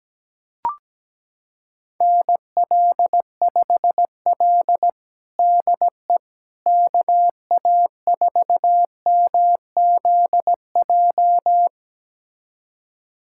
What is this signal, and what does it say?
Morse code, heard as a keyed tone, about 17 wpm, 700 Hz. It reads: NL5L DE KA4MZJ
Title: USA Callsign DE Callsign 17wpm